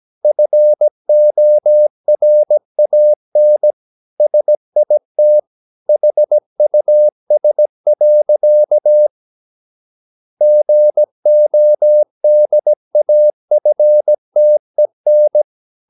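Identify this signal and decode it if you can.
Morse code, heard as a keyed tone, about 17 wpm, 600 Hz. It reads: FORAN SIT HUS. GODAFTEN
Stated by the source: Slow (QRS) Morse practice at 17 wpm, from CW med Gnister.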